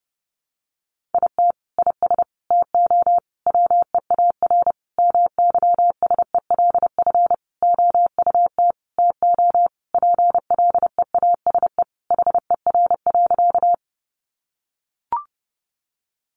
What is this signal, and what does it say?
Morse code, heard as a keyed tone, about 30 wpm, 700 Hz. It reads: IT IS TO WEAR MYSELF OUT TO PLEASE HER.
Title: Sentences from Top 1500 Words 30wpm